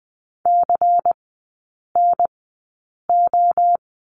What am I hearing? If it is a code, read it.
Morse code, heard as a keyed tone, about 20 wpm, spaced as for 8 wpm, 700 Hz. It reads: CNO
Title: 093 Lesson 16e - Letter c - Three Characters 20wpm